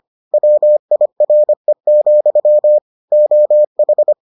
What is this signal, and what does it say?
Morse code, heard as a keyed tone, about 25 wpm, 600 Hz. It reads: WIRE, OH